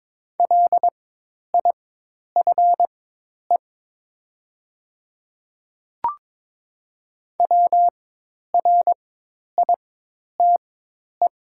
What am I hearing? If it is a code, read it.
Morse code, heard as a keyed tone, about 22 wpm, spaced as for 10 wpm, 700 Hz. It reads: LIFE WRITE
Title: 4x Character Spacing - Top 500 Words - 22wpm